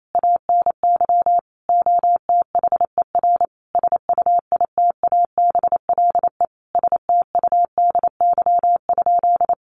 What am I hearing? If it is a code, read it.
Morse code, heard as a keyed tone, about 28 wpm, 700 Hz. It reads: ANY OTHER SUITABLE STUDY?
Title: Sentences from Top 1400 Words 28wpm